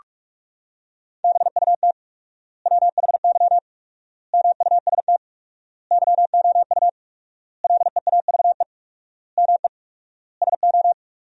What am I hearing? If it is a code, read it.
Morse code, heard as a keyed tone, about 46 wpm, 700 Hz. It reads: BUT WHY MUST YOU LEAVE ME SO
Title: 4x Word-Spacing - Sentences from Top 400 Words - 45wpm